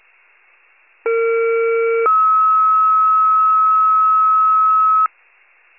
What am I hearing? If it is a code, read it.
Two-tone page, heard as a one-second tone, then a three-second tone, about 460 then 1290 Hz. It is heard from 1.0 to 5.1 s.
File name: wcalert